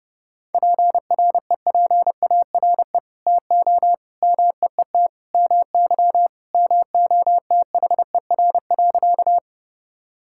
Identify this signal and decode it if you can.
Morse code, heard as a keyed tone, about 30 wpm, 700 Hz. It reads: PREPARE TO MEET MY MOTHER.
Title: Sentences from Top 1000 Words - Review Part 6 - 30wpm